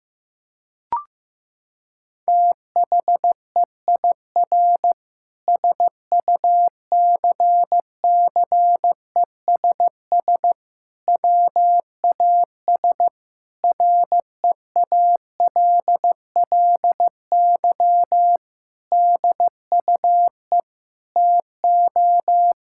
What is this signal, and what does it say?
Morse code, heard as a keyed tone, about 16 wpm, 700 Hz. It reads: THEIR SUCCESS WAS REALLY DUE TO
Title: Sentences from Top 1000 Words - Review Part 6 - 15wpm